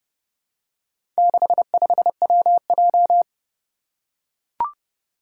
Morse code, heard as a keyed tone, about 30 wpm, 700 Hz. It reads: 65WJ
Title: Four Challenging Letter-Number Combinations 30wpm